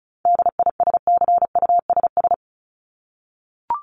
Morse code, heard as a keyed tone, about 34 wpm, 700 Hz. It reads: DISCUSS
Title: ED Suffix - Top 100 Words with Base Words 35wpm